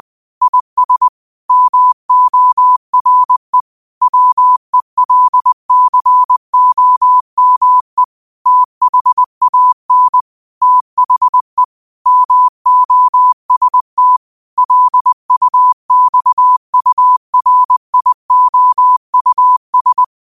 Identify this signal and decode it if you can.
Morse code, heard as a keyed tone, about 20 wpm, 1000 Hz. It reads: IS MORE WELCOME THAN THE MOST LUXURIOUS
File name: qotdfiles / 20wpm